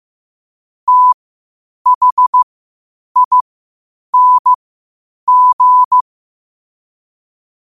Morse code, heard as a keyed tone, about 15 wpm, spaced as for 8 wpm, 1000 Hz. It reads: THING
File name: qotdfiles / 8wpm